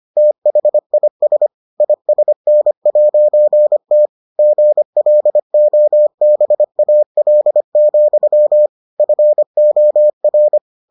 Morse code, heard as a keyed tone, about 25 wpm, 600 Hz. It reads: THIS ISN'T GLOBAL, FOR